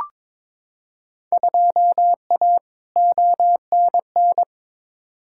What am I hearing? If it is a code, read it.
Morse code, heard as a keyed tone, about 22 wpm, 700 Hz. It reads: 2A ONN
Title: Field Day Exchanges Encore - 22wpm